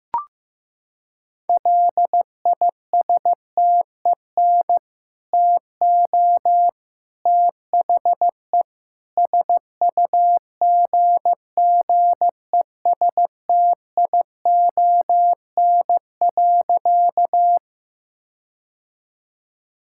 Morse code, heard as a keyed tone, about 15 wpm, 700 Hz. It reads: LISTEN TO THE SUGGESTION.
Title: Sentences from Top 1600 Words 15wpm